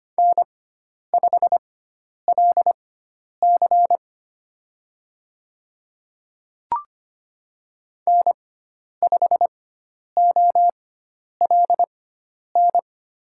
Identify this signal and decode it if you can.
Morse code, heard as a keyed tone, about 25 wpm, spaced as for 10 wpm, 700 Hz. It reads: N5LC N5OLN
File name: Lesson 16g - Letter c - Call Signs 25wpm